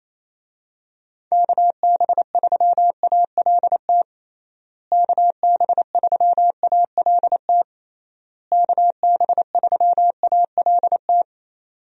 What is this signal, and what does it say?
Morse code, heard as a keyed tone, about 27 wpm, 700 Hz. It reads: KB3ALT KB3ALT KB3ALT
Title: USA Call Signs - 6 Characters - Repeated 3x - 28wpm